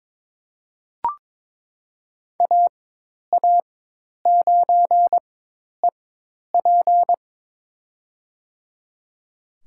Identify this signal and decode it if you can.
Morse code, heard as a keyed tone, about 21 wpm, spaced as for 10 wpm, 700 Hz. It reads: AA9EP
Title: USA Call Signs - 5 Characters - 4x Character Spacing - 22wpm